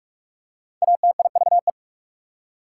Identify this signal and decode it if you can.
Morse code, heard as a keyed tone, about 47 wpm, 690 Hz. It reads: ATIVE
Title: Prefixes and Suffixes 45wpm